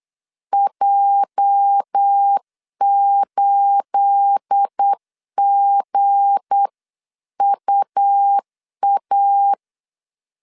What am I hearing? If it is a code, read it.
Morse code, heard as a keyed tone, about 8 wpm, 800 Hz. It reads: J8GUA
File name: CW / DET / 1 9 1500C S B 8wpm 5624